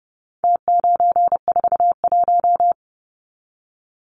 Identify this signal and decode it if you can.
Morse code, heard as a keyed tone, about 30 wpm, 700 Hz. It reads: T941
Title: Numbers - Contest Style - 4-Digit 30wpm